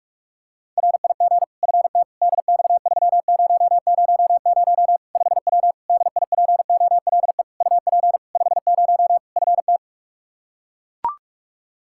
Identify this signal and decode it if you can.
Morse code, heard as a keyed tone, about 42 wpm, 700 Hz. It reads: RIG FT DX3000 5W DIPOLE UP 50 FT
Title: Practice QSO - Exchange 2 - Info - 45wpm